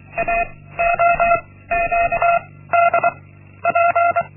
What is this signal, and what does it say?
Morse code, heard as a keyed tone, about 24 wpm, 690 Hz. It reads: A O Q D P
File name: Linksmakalnis-1